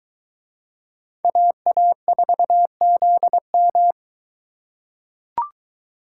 Morse code, heard as a keyed tone, about 22 wpm, 700 Hz. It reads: AA4ZM